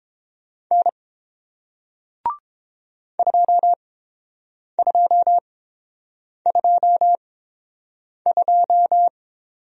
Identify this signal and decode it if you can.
Morse code, heard as a keyed tone, about 30 wpm, spaced as for 7 wpm, 700 Hz. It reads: N 2222